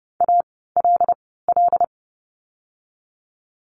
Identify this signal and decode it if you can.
Morse code, heard as a keyed tone, about 30 wpm, spaced as for 17 wpm, 700 Hz. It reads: ALL
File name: Lesson 17f - Letter u - Words 30wpm